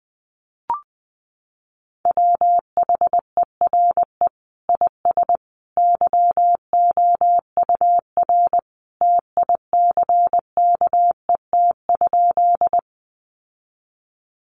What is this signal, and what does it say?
Morse code, heard as a keyed tone, about 20 wpm, 700 Hz. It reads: WHERE IS YOUR TICKET?
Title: Sentences from Top 1400 Words 20wpm